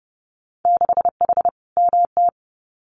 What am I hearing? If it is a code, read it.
Morse code, heard as a keyed tone, about 30 wpm, 700 Hz. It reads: 6H MT
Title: Winter Field Day Exchanges Encore 30wpm